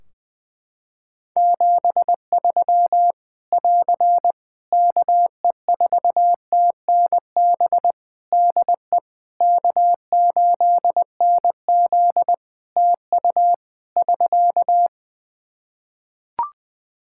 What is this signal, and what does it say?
Morse code, heard as a keyed tone, about 20 wpm, 700 Hz. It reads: 73 <AR> KE4TNB DE K8NZ TU <SK>
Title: Practice QSO - Exchange 4 - Ending - 20wpm